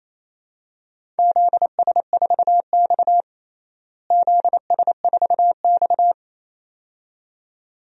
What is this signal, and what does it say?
Morse code, heard as a keyed tone, about 28 wpm, 700 Hz. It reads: ZS4X ZS4X
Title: DXCC-100 Call Signs - 4 Characters - Repeated 2x - 28wpm